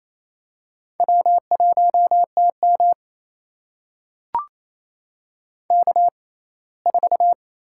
Morse code, heard as a keyed tone, about 28 wpm, spaced as for 19 wpm, 700 Hz. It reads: W1TM K4
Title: USA Call Signs - 6x Character Spacing - 28wpm